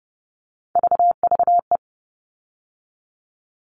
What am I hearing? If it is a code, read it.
Morse code, heard as a keyed tone, about 30 wpm, 700 Hz. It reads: VVE
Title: All Cut Numbers - 3-Digit 30wpm